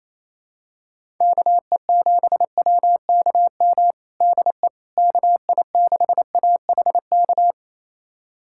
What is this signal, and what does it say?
Morse code, heard as a keyed tone, about 28 wpm, 700 Hz. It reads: KE7WKM DE KI6AHK
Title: USA Callsign DE Callsign - 6 Characters - 28wpm